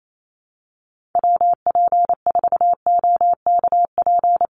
Morse code, heard as a keyed tone, about 28 wpm, 700 Hz. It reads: WP4OKP
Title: USA Call Signs 28wpm